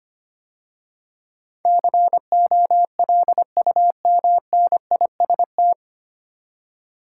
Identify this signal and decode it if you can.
Morse code, heard as a keyed tone, about 25 wpm, 690 Hz. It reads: COLUMNIST
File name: practice / 25wpm / 9 Letter Words - First 3 Letters Given - 25wpm